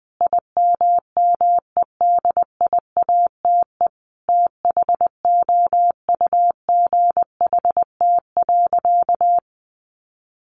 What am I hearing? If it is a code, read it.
Morse code, heard as a keyed tone, about 20 wpm, 700 Hz. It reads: IMMEDIATE THOUGHT.
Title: Sentences from Top 1500 Words 20wpm